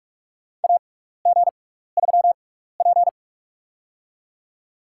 Morse code, heard as a keyed tone, about 47 wpm, spaced as for 16 wpm, 700 Hz. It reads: AG3P
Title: USA Call Signs - 4 Characters - 6x Character Spacing - 45wpm